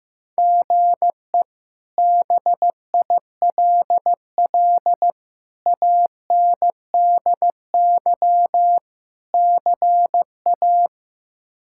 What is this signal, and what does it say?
Morse code, heard as a keyed tone, about 15 wpm, 700 Hz. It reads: GE BILL ANDY CA
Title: K1USN SST QSO Practice - 15wpm